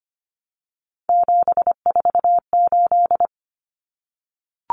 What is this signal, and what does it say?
Morse code, heard as a keyed tone, about 25 wpm, 700 Hz. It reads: 748
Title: Common Cut Numbers - 3-Digit 25wpm